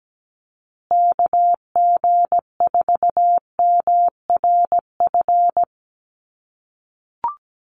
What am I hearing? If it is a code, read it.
Morse code, heard as a keyed tone, about 17 wpm, 700 Hz. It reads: KG4MRF
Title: USA Call Signs Repeated 2x 17wpm